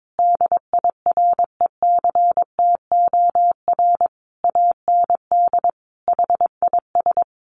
Morse code, heard as a keyed tone, about 22 wpm, 700 Hz. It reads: DIRECTOR AND HIS
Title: Sentences from Top 1300 Words 22wpm